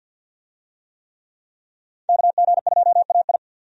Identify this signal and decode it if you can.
Morse code, heard as a keyed tone, about 47 wpm, 700 Hz. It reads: KG2AI
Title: USA Call Signs - 5 Characters - 50wpm